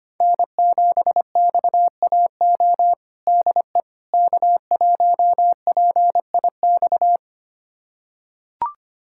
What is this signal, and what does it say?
Morse code, heard as a keyed tone, about 25 wpm, 700 Hz. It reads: N7XAO DE K1PIX
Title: USA Callsign DE Callsign - 5 Characters - 25wpm